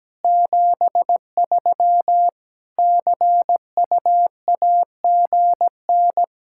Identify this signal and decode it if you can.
Morse code, heard as a keyed tone, about 17 wpm, 700 Hz. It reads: 73 CUAGN
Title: Practice QSO - Exchange 4 - Ending - 17wpm